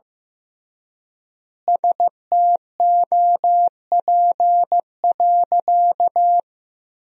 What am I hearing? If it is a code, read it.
Morse code, heard as a keyed tone, about 15 wpm, 700 Hz. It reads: STOP.
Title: 3x Word-Spacing - Sentences from Top 300 Words - 15wpm